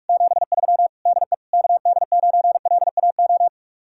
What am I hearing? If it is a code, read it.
Morse code, heard as a keyed tone, about 57 wpm, 700 Hz. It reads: 73 DE KD9LAO